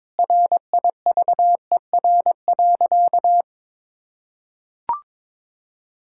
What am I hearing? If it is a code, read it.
Morse code, heard as a keyed tone, about 22 wpm, 700 Hz. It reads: RIVER.
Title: Sentences from Top 1000 Words - Review Part 6 - 22wpm